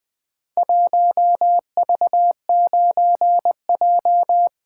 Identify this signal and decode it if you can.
Morse code, heard as a keyed tone, about 20 wpm, 700 Hz. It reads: 1V9J
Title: Four Letter-Number Combinations - 20wpm